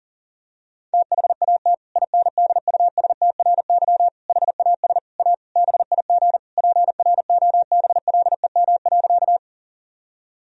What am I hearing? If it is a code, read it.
Morse code, heard as a keyed tone, about 40 wpm, 700 Hz. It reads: THAT INDUSTRY HAS A BIG PROBLEM.
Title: Sentences from Top 1000 Words - Review Part 4 - 40wpm